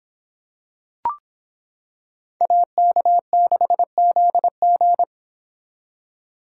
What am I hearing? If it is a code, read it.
Morse code, heard as a keyed tone, about 26 wpm, 700 Hz. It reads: AK6ZG